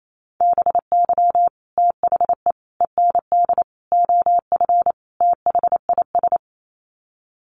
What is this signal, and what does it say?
Morse code, heard as a keyed tone, about 28 wpm, 700 Hz. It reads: BY THE END OF THIS
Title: Sets of 5 Words Encore 28wpm